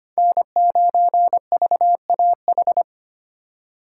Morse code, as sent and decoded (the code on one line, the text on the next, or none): -. ----. ...- .- ....
N9VAH